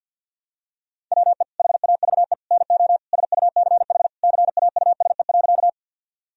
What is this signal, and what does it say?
Morse code, heard as a keyed tone, about 46 wpm, 700 Hz. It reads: WE HAVE NO SUCH CAUSE.
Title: Sentences from Top 1000 Words - Review Part 1 - 50wpm